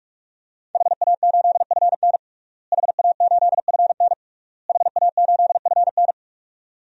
Morse code, heard as a keyed tone, about 46 wpm, 700 Hz. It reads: HA8FN HA8FN HA8FN
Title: DXCC-100 Call Signs - 5 Characters - Repeated 3x - 45wpm